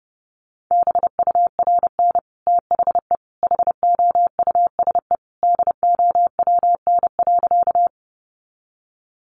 Morse code, heard as a keyed tone, about 30 wpm, 700 Hz. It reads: BURN THE HOUSE DOWN.
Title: Sentences from Top 1300 Words 30wpm